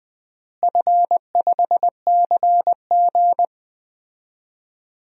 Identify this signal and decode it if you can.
Morse code, heard as a keyed tone, about 20 wpm, 700 Hz. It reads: F5CG
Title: DXCC-100 Call Signs - 4 Characters - Repeated 3x - 20wpm